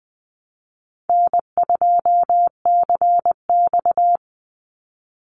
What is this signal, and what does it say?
Morse code, heard as a keyed tone, about 19 wpm, 700 Hz. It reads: N2CX
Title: Easy Call Signs Repeated 2x - 20wpm